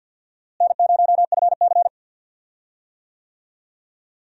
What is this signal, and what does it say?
Morse code, heard as a keyed tone, about 51 wpm, 700 Hz. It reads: N0FC
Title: USA Call Signs - 4 Characters - Repeated 2x - 50wpm